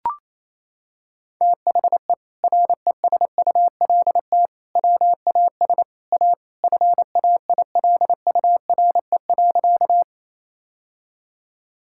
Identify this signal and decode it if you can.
Morse code, heard as a keyed tone, about 28 wpm, 700 Hz. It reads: THE RESULT WAS A FAILURE.
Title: Sentences from Top 1000 Words - Review Part 6 - 28wpm